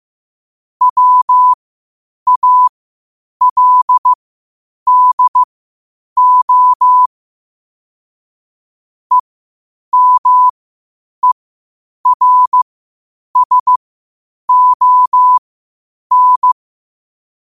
Morse code, heard as a keyed tone, about 15 wpm, spaced as for 8 wpm, 1000 Hz. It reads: WALDO EMERSON
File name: qotdfiles / 8wpm